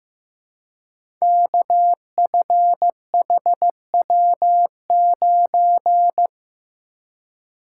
Morse code, heard as a keyed tone, about 15 wpm, 700 Hz. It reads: KFHW9
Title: Code Groups - 5 Letter:Numbers - 15wpm